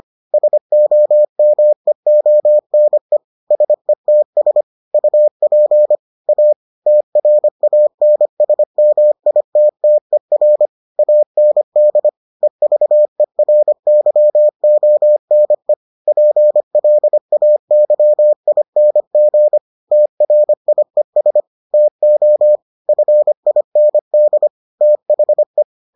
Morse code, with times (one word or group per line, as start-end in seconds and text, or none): 0.3-3.2 SOMEONE
3.5-4.6 SETS
4.9-6.0 UP
6.3-6.5 A
6.9-10.7 TRANSMITTER
11.0-12.1 AND
12.4-15.7 EVERYONE
16.1-19.6 PLAYING
19.9-21.4 TRIES
21.7-22.6 TO
22.9-24.5 FIND
24.8-25.6 THE